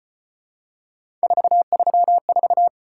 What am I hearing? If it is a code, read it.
Morse code, heard as a keyed tone, about 33 wpm, 700 Hz. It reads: 434